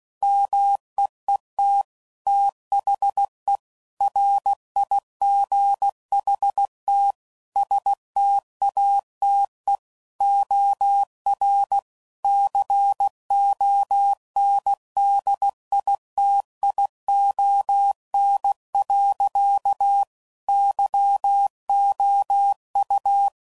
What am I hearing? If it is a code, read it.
Morse code, heard as a keyed tone, about 16 wpm, 800 Hz. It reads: MEET THE RIGHT STATE OR CONDITION. YOU
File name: CW Code Groups @15WPM / Technical Subjects / OR Gate